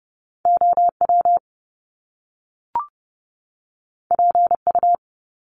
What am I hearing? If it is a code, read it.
Morse code, heard as a keyed tone, about 30 wpm, 700 Hz. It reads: OW PU
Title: Top Two Letter Combinations in Words 30wpm